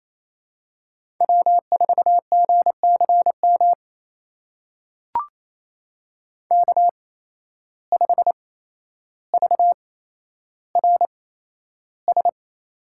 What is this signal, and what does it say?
Morse code, heard as a keyed tone, about 28 wpm, spaced as for 12 wpm, 700 Hz. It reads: W4GCM K5VRS
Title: USA Call Signs - 5 Characters - 8x Character Spacing - 28wpm